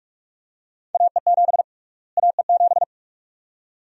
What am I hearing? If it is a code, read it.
Morse code, heard as a keyed tone, about 46 wpm, 700 Hz. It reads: AE7 AE7
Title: USA Call Sign Prefixes - Repeated 2x - 45wpm